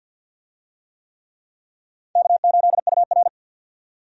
Morse code, heard as a keyed tone, about 50 wpm, 700 Hz. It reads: K8UR
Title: USA Call Signs - 4 Characters - 50wpm